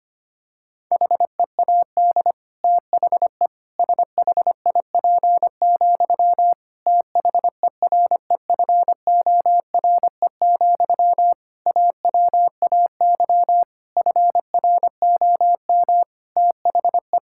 Morse code, heard as a keyed tone, about 25 wpm, 700 Hz. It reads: HEAD THE SHIP, THEREFORE, AWAY FROM THE
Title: Sentences from Top 1000 Words - Review Part 5 - 25wpm